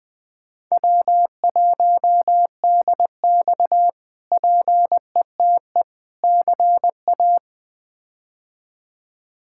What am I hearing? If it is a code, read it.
Morse code, heard as a keyed tone, about 21 wpm, 700 Hz. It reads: W1DX PETE CA
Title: K1USN SST QSO Practice - 20wpm